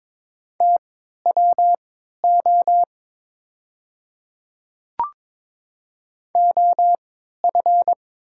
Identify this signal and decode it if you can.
Morse code, heard as a keyed tone, about 22 wpm, spaced as for 13 wpm, 690 Hz. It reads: TWO OF
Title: 3x Character Spacing - Top 100 Words - 22wpm